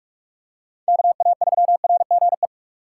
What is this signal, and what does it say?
Morse code, heard as a keyed tone, about 46 wpm, 700 Hz. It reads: KA3RGE
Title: USA Call Signs - 6 Characters - 4x Character Spacing - 45wpm